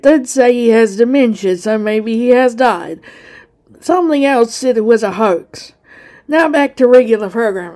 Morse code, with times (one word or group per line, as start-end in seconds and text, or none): none